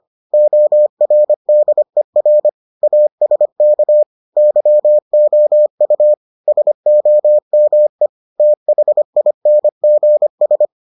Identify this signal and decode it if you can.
Morse code, heard as a keyed tone, about 25 wpm, 600 Hz. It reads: ORDER ASK YOU SOME THINGS